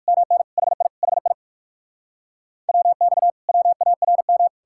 Morse code, heard as a keyed tone, about 59 wpm, 700 Hz. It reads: MN HI HI WX WARM